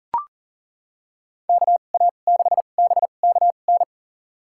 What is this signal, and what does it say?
Morse code, heard as a keyed tone, about 40 wpm, spaced as for 29 wpm, 700 Hz. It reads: KA6BKN
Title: USA Call Signs - 6 Characters - 2x Character Spacing - 40wpm